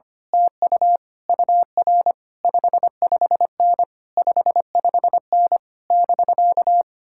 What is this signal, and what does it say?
Morse code, heard as a keyed tone, about 24 wpm, 700 Hz. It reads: TU UR 55N 55N <BK>
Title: Speed-Racing Common QSO Phrases Encore - 25wpm